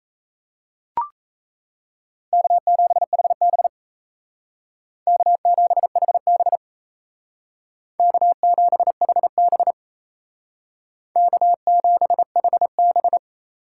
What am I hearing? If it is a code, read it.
Morse code, heard as a keyed tone, about 35 wpm, 700 Hz. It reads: K7HB K7HB K7HB K7HB